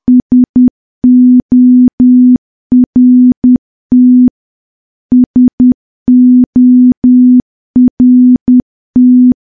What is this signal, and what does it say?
Morse code, heard as a keyed tone, about 10 wpm, 250 Hz. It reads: SORT SORT
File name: morse / morse-training-10WPM